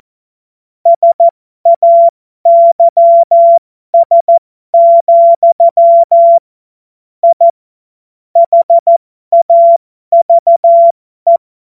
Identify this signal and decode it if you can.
Morse code, heard as a keyed tone, about 14 wpm, spaced as for 12 wpm, 680 Hz. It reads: SAYS, I HAVE
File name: Morsekurs / Witze